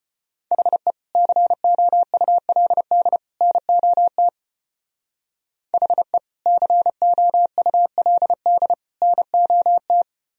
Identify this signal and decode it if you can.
Morse code, heard as a keyed tone, about 30 wpm, 700 Hz. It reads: HE COULD NOT HE COULD NOT
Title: Speed-Racing - Sets of 3 Words 25wpm